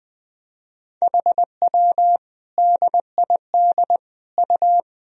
Morse code, heard as a keyed tone, about 20 wpm, 700 Hz. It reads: HW DID U
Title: Sentences from Top 100 Words and Abbreviations 20wpm